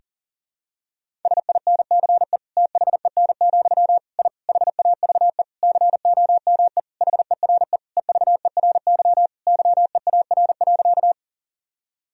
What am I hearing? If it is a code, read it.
Morse code, heard as a keyed tone, about 37 wpm, 700 Hz. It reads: SINCE THEN, I HAVE COME HERE EVERY YEAR.
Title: Sentences from Top 1000 Words - Review Part 1 - 40wpm